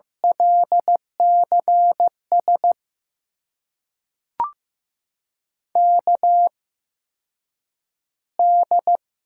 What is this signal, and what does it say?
Morse code, heard as a keyed tone, about 16 wpm, spaced as for 8 wpm, 700 Hz. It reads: LCS KD
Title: USA Call Signs - 6 Characters - 8x Character Spacing - 15wpm